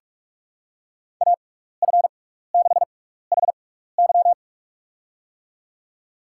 Morse code, heard as a keyed tone, about 47 wpm, spaced as for 16 wpm, 700 Hz. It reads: AF6HY
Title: USA Call Signs - 5 Characters - 6x Character Spacing - 45wpm